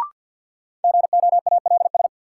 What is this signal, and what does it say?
Morse code, heard as a keyed tone, about 51 wpm, 700 Hz. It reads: GOALS
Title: Sentences from Top 1000 Words - Review Part 6 - New Word Drill - 50wpm